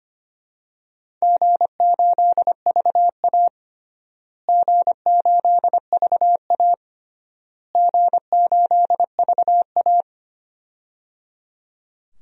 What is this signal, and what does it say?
Morse code, heard as a keyed tone, about 25 wpm, 700 Hz. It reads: G8VA G8VA G8VA
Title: DXCC-100 Call Signs - 4 Characters - Repeated 3x - 25wpm